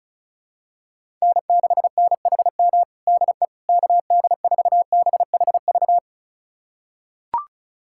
Morse code, heard as a keyed tone, about 36 wpm, 700 Hz. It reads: N6NHM DE KD4BHV
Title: USA Callsign DE Callsign 35wpm